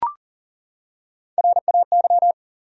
Morse code, heard as a keyed tone, about 40 wpm, 700 Hz. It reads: RAY
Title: USA Common Names - Max 4 Letters - 40wpm